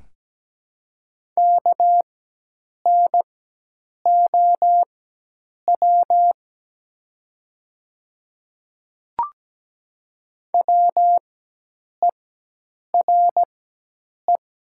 Morse code, heard as a keyed tone, about 17 wpm, spaced as for 8 wpm, 700 Hz. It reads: KNOW WERE
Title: 4x Character Spacing - Top 500 Words - 17wpm